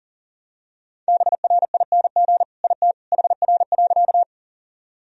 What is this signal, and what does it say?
Morse code, heard as a keyed tone, about 40 wpm, 700 Hz. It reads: BRING IT HR.
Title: Sentences from Top 500 Words and Abbreviations 40wpm